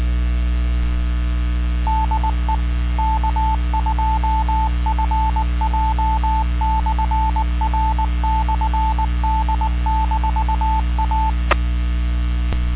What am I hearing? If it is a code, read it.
Morse code, heard as a keyed tone, about 19 wpm, 880 Hz, with radio noise. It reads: DE K2FJ/R/D-A